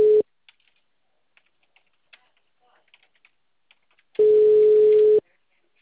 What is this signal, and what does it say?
Ringback tone (Europe, 425 Hz).